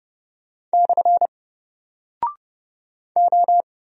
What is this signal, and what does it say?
Morse code, heard as a keyed tone, about 30 wpm, 700 Hz. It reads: / O
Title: Single Character (Letter:Number:Punctuation:Prosign) 30wpm